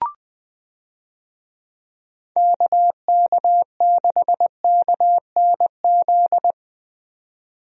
Morse code, heard as a keyed tone, about 20 wpm, 700 Hz. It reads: KK6KNZ
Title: USA Call Signs - 6 Characters - 20wpm